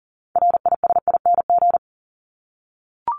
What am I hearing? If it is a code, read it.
Morse code, heard as a keyed tone, about 39 wpm, 700 Hz. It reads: RISING
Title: ING Suffix - Top 100 Words with Base Words 40wpm